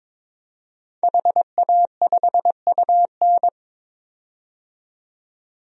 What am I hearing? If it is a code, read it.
Morse code, heard as a keyed tone, about 22 wpm, 700 Hz. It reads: HA5UN
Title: DXCC-100 Call Signs - 5 Characters - 22wpm